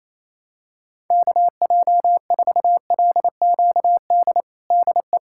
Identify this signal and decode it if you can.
Morse code, heard as a keyed tone, about 28 wpm, 700 Hz. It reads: KJ4LQD DE
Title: USA Callsign DE Callsign - 6 Characters - 28wpm